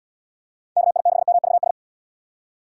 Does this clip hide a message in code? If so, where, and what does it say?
Yes: Morse code from 0.8 to 1.7 s, heard as a keyed tone, about 75 wpm, 700 Hz. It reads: KI6MVS